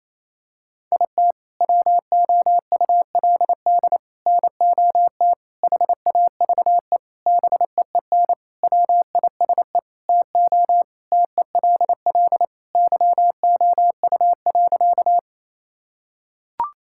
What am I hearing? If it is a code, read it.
Morse code, heard as a keyed tone, about 28 wpm, 700 Hz. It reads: IT WOULD NOT HAVE BEEN WISE TO TELL YOU.
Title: Sentences from Top 1600 Words 28wpm